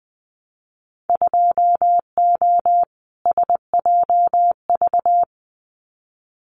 Morse code, heard as a keyed tone, about 19 wpm, 700 Hz. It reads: 2O SJV
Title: Winter Field Day Exchanges Encore 20wpm